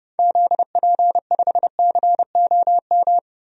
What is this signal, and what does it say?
Morse code, heard as a keyed tone, about 32 wpm, 700 Hz. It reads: ZP5COM
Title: DXCC-100 Call Signs - 6 Characters - 30wpm